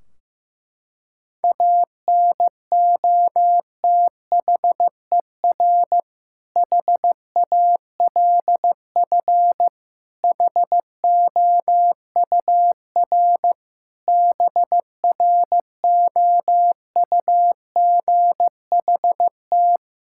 Morse code, heard as a keyed tone, about 15 wpm, 700 Hz. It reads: ANOTHER HALF HOUR BROUGHT
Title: Sentences from Top 1600 Words 15wpm